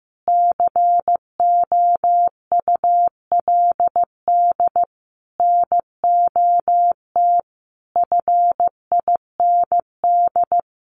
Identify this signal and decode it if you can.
Morse code, heard as a keyed tone, about 15 wpm, 700 Hz. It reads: COULD NOT FIND
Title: Sentences from Top 1400 Words 15wpm